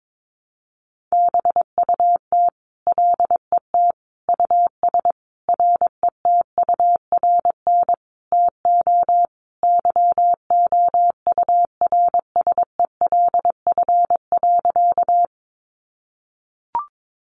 Morse code, heard as a keyed tone, about 22 wpm, 700 Hz. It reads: BUT LET US RETURN TO YOURSELF.
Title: Sentences from Top 800 Words 22wpm